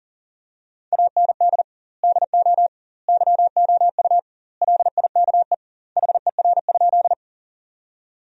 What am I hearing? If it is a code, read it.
Morse code, heard as a keyed tone, about 40 wpm, 700 Hz. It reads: AND DO YOU LIKE HER?
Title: 2x Word-Spacing - Sentences from Top 100 Words - 40wpm